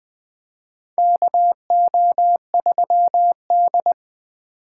Morse code, heard as a keyed tone, about 20 wpm, 700 Hz. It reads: KO3D
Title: USA Call Signs - 4 Characters - 4x Character Spacing - 20wpm